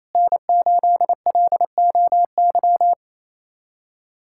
Morse code, heard as a keyed tone, about 27 wpm, 700 Hz. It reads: N8LOY
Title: USA Callsign DE Callsign - 5 Characters - 28wpm